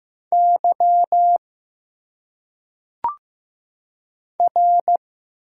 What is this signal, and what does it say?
Morse code, heard as a keyed tone, about 15 wpm, 700 Hz. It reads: Y R
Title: Single Letters - Phonetic Pronunciation 15wpm